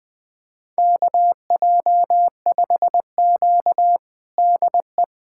Morse code, heard as a keyed tone, about 21 wpm, 700 Hz. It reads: KJ5Q DE
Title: USA Callsign DE Callsign - 4 Characters - 20wpm